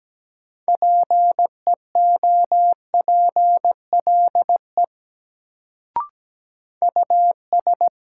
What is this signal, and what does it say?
Morse code, heard as a keyed tone, about 17 wpm, 700 Hz. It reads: PEOPLE US
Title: Sentences from Top 200 Words - New Word Drill - 17wpm